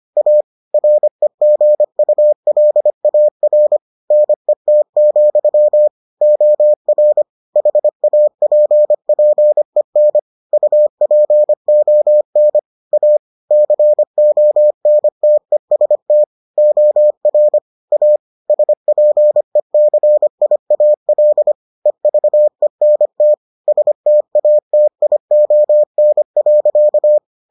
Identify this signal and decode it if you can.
Morse code, heard as a keyed tone, about 25 wpm, 600 Hz. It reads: A REGULAR NET, OR HAPPEN UPON A CONTEST OR A SPECIAL EVENT STATION.